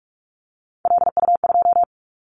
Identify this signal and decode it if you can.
Morse code, heard as a keyed tone, about 46 wpm, 700 Hz. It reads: LU2
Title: Callsign Prefixes for Top 50 DXCC - 45wpm